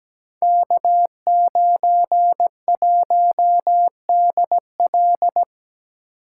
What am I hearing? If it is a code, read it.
Morse code, heard as a keyed tone, about 17 wpm, 700 Hz. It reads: K91DL
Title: Code Groups - 5 Letter:Numbers - 17wpm